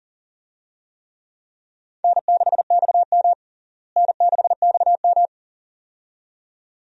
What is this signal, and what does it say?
Morse code, heard as a keyed tone, about 40 wpm, 700 Hz. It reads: N6XM N6XM
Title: USA Call Signs - 4 Characters - Repeated 2x - 40wpm